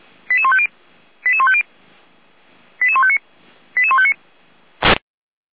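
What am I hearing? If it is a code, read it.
Five-tone selective call (ZVEI), heard as quick runs of five tones, tones of about 70 ms. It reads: 88150 88150 88159 88159
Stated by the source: ZVEI five-tone call received on SDR.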